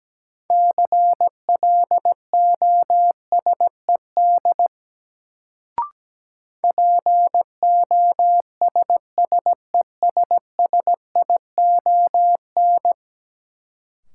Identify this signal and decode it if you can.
Morse code, heard as a keyed tone, about 17 wpm, 700 Hz. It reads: CLOSED POSSESSION
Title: Sentences from Top 1400 Words - New Word Drill - 17wpm